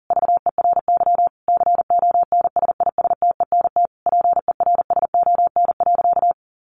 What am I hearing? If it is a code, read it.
Morse code, heard as a keyed tone, about 40 wpm, 700 Hz. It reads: VERY CONSISTENT PERSON.
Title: Sentences from Top 1500 Words 40wpm